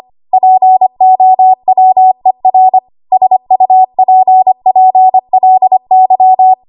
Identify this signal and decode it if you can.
Morse code, heard as a keyed tone, about 25 wpm, 750 Hz. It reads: POWER SUPPLY
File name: Morse / 25WPM